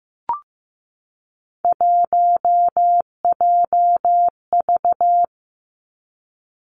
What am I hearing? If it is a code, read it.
Morse code, heard as a keyed tone, about 15 wpm, 700 Hz. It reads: 1JV
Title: Three Challenging Letter-Number Combinations 15wpm